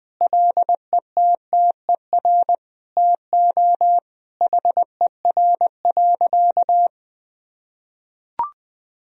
Morse code, heard as a keyed tone, about 19 wpm, 700 Hz. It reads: LETTER TO HER.
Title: Sentences from Top 1000 Words - Review Part 3 - 20wpm